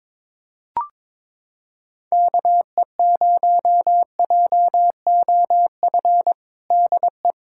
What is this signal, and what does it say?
Morse code, heard as a keyed tone, about 22 wpm, 700 Hz. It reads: KE0JOF DE
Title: USA Callsign DE Callsign 22wpm